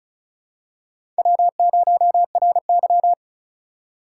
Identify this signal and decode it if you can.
Morse code, heard as a keyed tone, about 36 wpm, 700 Hz. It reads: W0RY
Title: USA Call Signs - 4 Characters - 2x Character Spacing - 35wpm